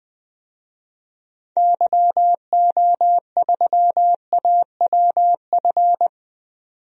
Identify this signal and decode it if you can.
Morse code, heard as a keyed tone, about 20 wpm, 700 Hz. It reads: YO3AWF
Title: DXCC-100 Call Signs - 6 Characters - 20wpm